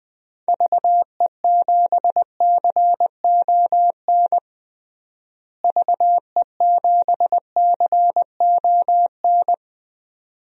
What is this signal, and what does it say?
Morse code, heard as a keyed tone, about 20 wpm, 700 Hz. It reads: VE7CON VE7CON
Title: DXCC-100 Call Signs - 6 Characters - Repeated 2x - 20wpm